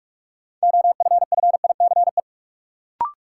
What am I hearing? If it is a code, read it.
Morse code, heard as a keyed tone, about 42 wpm, 700 Hz. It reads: OFFICE
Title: Sentences from Top 1000 Words - New Word Drill - 45wpm